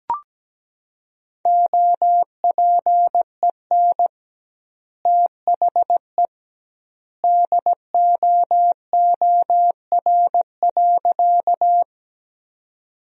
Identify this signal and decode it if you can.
Morse code, heard as a keyed tone, about 17 wpm, 700 Hz. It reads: OPEN THE DOOR.
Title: 2x Word-Spacing - Sentences from Top 300 Words - 17wpm